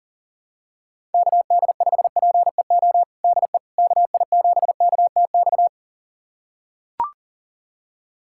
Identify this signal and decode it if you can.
Morse code, heard as a keyed tone, about 40 wpm, 700 Hz. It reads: KD5PEO DE KI7KTX
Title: USA Callsign DE Callsign - 6 Characters - 40wpm